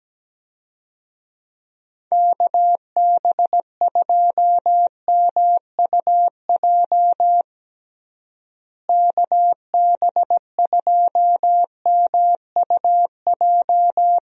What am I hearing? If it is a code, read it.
Morse code, heard as a keyed tone, about 17 wpm, 700 Hz. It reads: KB2MUJ KB2MUJ
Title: USA Call Signs - 6 Characters - Repeated 3x - 17wpm